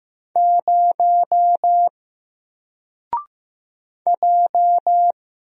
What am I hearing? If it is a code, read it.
Morse code, heard as a keyed tone, about 15 wpm, 700 Hz. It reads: 0 J
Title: Single Letter-Number 15wpm